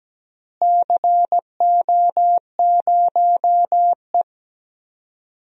Morse code, heard as a keyed tone, about 17 wpm, 700 Hz. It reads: CO0E
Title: DXCC-100 Call Signs - 4 Characters - Repeated 3x - 17wpm